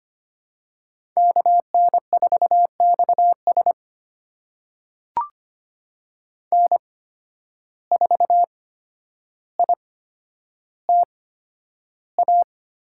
Morse code, heard as a keyed tone, about 25 wpm, spaced as for 11 wpm, 700 Hz. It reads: KN4XS N4ITA
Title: USA Call Signs - 5 Characters - 8x Character Spacing - 25wpm